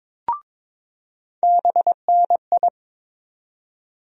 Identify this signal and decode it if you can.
Morse code, heard as a keyed tone, about 22 wpm, 700 Hz. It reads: BNI
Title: USA Call Sign Postfixes - 3 Letters 22wpm